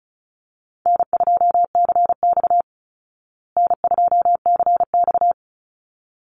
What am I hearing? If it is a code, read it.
Morse code, heard as a keyed tone, about 34 wpm, 700 Hz. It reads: N2CX N2CX
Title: Easy Call Signs Repeated 2x - 35wpm